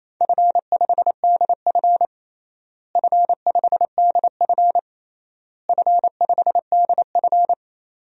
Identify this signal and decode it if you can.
Morse code, heard as a keyed tone, about 28 wpm, 700 Hz. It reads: F5DF F5DF F5DF
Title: DXCC-100 Call Signs - 4 Characters - Repeated 3x - 28wpm